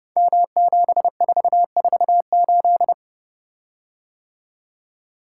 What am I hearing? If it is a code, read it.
Morse code, heard as a keyed tone, about 30 wpm, 700 Hz. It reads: M7448
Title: Code Groups - 5 Letter:Numbers - 30wpm